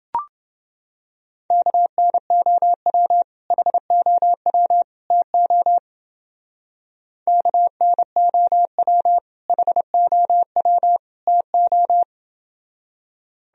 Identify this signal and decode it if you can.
Morse code, heard as a keyed tone, about 30 wpm, 700 Hz. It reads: KNOW HOW TO KNOW HOW TO